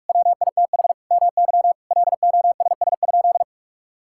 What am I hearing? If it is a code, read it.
Morse code, heard as a keyed tone, about 46 wpm, 700 Hz. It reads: WITH MY LOSS?
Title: Sentences from Top 1600 Words 45wpm